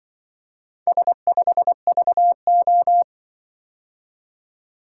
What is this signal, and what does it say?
Morse code, heard as a keyed tone, about 23 wpm, 700 Hz. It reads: S5VO
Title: Speed-Racing - Four Letter-Number Combinations - 20wpm